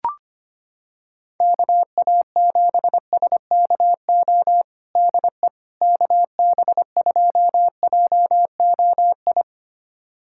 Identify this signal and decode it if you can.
Morse code, heard as a keyed tone, about 25 wpm, 700 Hz. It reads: KA7SKO DE KB2JOI
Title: USA Callsign DE Callsign - 6 Characters - 25wpm